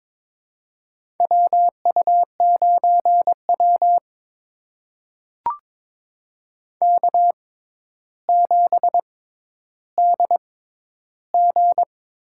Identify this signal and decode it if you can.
Morse code, heard as a keyed tone, about 22 wpm, spaced as for 11 wpm, 700 Hz. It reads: WU9W K7DG
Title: USA Call Signs - 6x Character Spacing - 22wpm